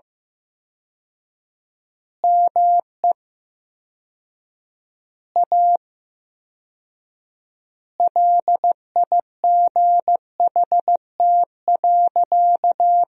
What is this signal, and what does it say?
Morse code, heard as a keyed tone, about 14 wpm, 700 Hz. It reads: ME A LIGHT.
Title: 4x Word-Spacing - Sentences from Top 200 Words - 15wpm